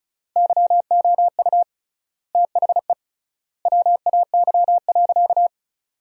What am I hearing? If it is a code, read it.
Morse code, heard as a keyed tone, about 36 wpm, 700 Hz. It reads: YOU THE WAY.
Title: 3x Word-Spacing - Sentences from Top 300 Words - 35wpm